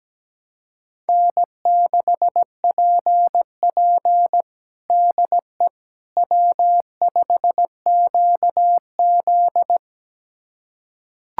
Morse code, heard as a keyed tone, about 17 wpm, 700 Hz. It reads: N6PP DE W5QZ
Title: USA Callsign DE Callsign - 4 Characters - 17wpm